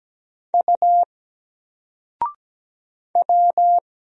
Morse code, heard as a keyed tone, about 17 wpm, 700 Hz. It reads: U W
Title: Single Letter-Number 17wpm